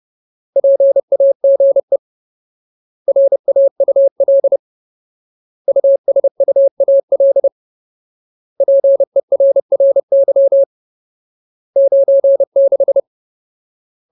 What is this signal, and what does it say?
Morse code, heard as a keyed tone, about 30 wpm, 550 Hz. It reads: PAGE RAUL USUAL PERRY 96
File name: Ultimate Word List / 30WPM Ultimate